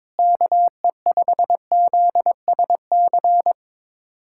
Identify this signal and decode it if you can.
Morse code, heard as a keyed tone, about 22 wpm, 700 Hz. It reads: KE5ZSC
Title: USA Callsign DE Callsign 22wpm